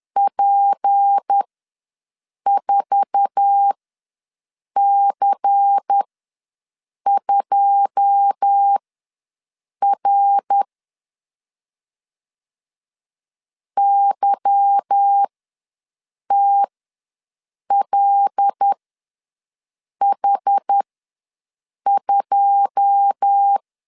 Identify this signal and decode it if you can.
Morse code, heard as a keyed tone, about 10 wpm, spaced as for 6 wpm, 800 Hz. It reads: P4C2R YTLH2
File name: LEZIONI / 9 2 ws 10wpm 4130